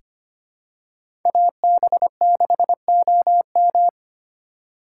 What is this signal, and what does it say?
Morse code, heard as a keyed tone, about 25 wpm, 700 Hz. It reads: AB6OM
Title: USA Call Signs - 5 Characters - 3x Character Spacing - 25wpm